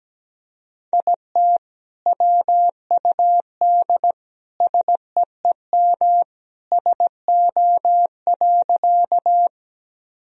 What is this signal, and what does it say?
Morse code, heard as a keyed tone, about 17 wpm, 700 Hz. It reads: IT WUD SEEM SO.
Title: Sentences from Top 300 Words and Abbreviations 17wpm